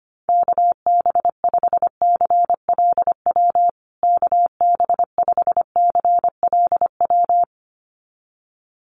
Morse code, heard as a keyed tone, about 25 wpm, 700 Hz. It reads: KB5CLW KB5CLW
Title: USA Call Signs Repeated 2x 25wpm